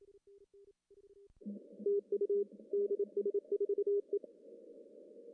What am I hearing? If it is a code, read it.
Morse code, heard as a keyed tone, about 28 wpm, 400 Hz. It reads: TU DS4E
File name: CW / cqww-cw-k5tr